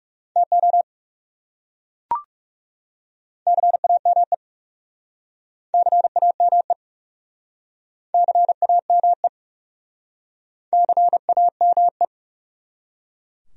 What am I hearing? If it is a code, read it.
Morse code, heard as a keyed tone, about 37 wpm, 700 Hz. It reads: TO CAME CAME CAME CAME
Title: Speed-Racing Top 300 Words 30wpm